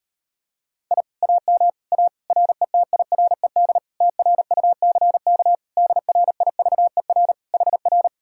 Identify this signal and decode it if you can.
Morse code, heard as a keyed tone, about 38 wpm, 700 Hz. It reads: I AM A RETIRED TRUCK DRIVER HR